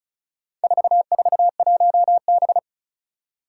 Morse code, heard as a keyed tone, about 34 wpm, 700 Hz. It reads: 441B
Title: Four Challenging Letter-Number Combinations 35wpm